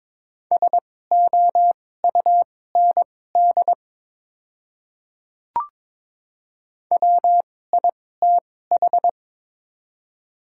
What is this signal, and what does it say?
Morse code, heard as a keyed tone, about 22 wpm, spaced as for 16 wpm, 690 Hz. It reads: SOUND WITH
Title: 2x Character Spacing - Top 100 Words - 22wpm